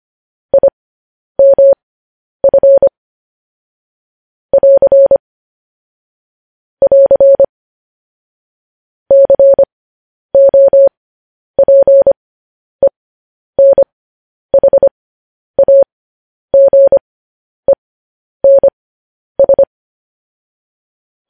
Morse code, heard as a keyed tone, about 25 wpm, spaced as for 10 wpm, 550 Hz. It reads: IMF <AR> <AR> COPENHAGENS
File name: podcast / News headlines-1x-25x10